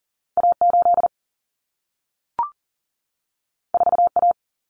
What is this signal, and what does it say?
Morse code, heard as a keyed tone, about 40 wpm, 700 Hz. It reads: A8 4A
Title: Common Cut Numbers - 2-Digit 40wpm